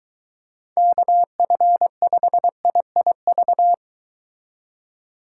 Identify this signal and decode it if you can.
Morse code, heard as a keyed tone, about 23 wpm, 700 Hz. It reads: KF5IIV